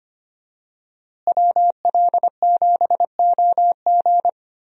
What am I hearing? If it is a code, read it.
Morse code, heard as a keyed tone, about 25 wpm, 700 Hz. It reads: WL7OG
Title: USA Call Signs - 5 Characters - Repeated 2x - 25wpm